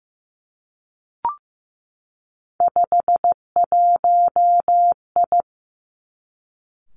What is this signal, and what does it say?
Morse code, heard as a keyed tone, about 15 wpm, 700 Hz. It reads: 51I
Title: Three Challenging Letter-Number Combinations 15wpm